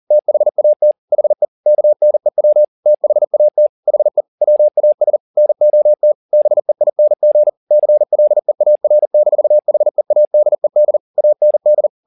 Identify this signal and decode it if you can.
Morse code, heard as a keyed tone, about 40 wpm, 600 Hz. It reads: THAT HE KNEW THAT HE WAS NOT BEING CLEAR-HEADED AND
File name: the old man and the sea 2 40wpm